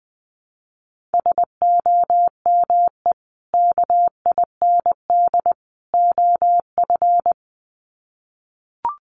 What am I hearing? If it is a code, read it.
Morse code, heard as a keyed tone, about 20 wpm, 700 Hz. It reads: SOME KIND OF
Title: Sets of 3 Words 20wpm